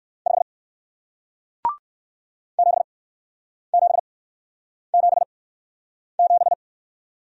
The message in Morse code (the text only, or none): 5 7777